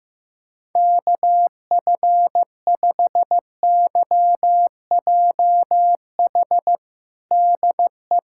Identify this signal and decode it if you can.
Morse code, heard as a keyed tone, about 15 wpm, 700 Hz. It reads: KF5YJH DE
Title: USA Callsign DE Callsign - 6 Characters - 15wpm